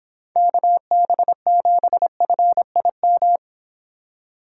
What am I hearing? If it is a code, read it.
Morse code, heard as a keyed tone, about 25 wpm, 700 Hz. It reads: KB7FIM